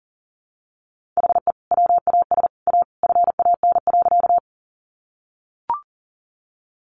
Morse code, heard as a keyed tone, about 40 wpm, 700 Hz. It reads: HE WAS A FAN.
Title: Sentences from Top 1300 Words 40wpm